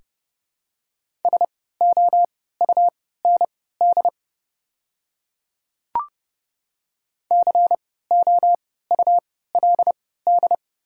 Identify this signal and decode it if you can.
Morse code, heard as a keyed tone, about 30 wpm, spaced as for 17 wpm, 700 Hz. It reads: SOUND COULD
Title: 3x Character Spacing - Top 100 Words - 30wpm